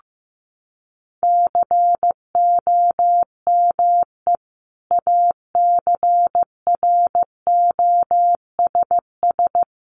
Morse code, heard as a keyed tone, about 15 wpm, 700 Hz. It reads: COME ACROSS
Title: Sentences from Top 700 Words 15wpm